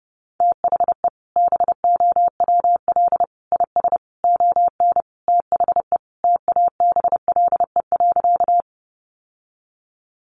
Morse code, heard as a keyed tone, about 30 wpm, 700 Hz. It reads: THE BOWL IS ON THE TABLE.
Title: Sentences from Top 1200 Words 30wpm